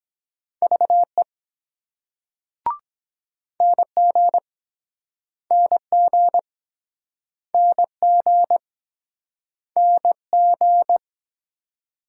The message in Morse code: VE NG NG NG NG

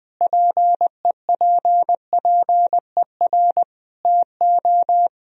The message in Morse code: PEPPER TO

